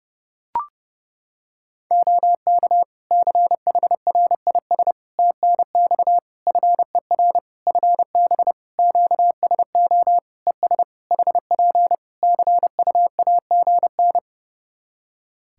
Morse code, heard as a keyed tone, about 30 wpm, 700 Hz. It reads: OK CHRIS TNX FER FB QSO ES HP CUAGN
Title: Practice QSO - Exchange 4 - Ending - 30wpm